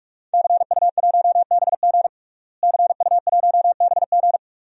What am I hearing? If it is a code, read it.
Morse code, heard as a keyed tone, about 43 wpm, 700 Hz. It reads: CU1BG CU1BG
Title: DXCC-100 Call Signs - 5 Characters - Repeated 2x - 45wpm